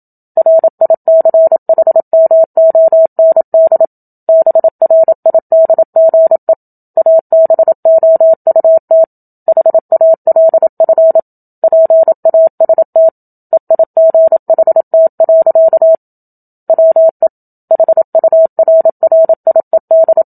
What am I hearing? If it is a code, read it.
Morse code, heard as a keyed tone, about 27 wpm, 650 Hz. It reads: RICHMOND BRIDGE ABOUT HALF PAST EIGHT. WE HURRIED
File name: Book Two / 01-Chapter 1 - 27 WPM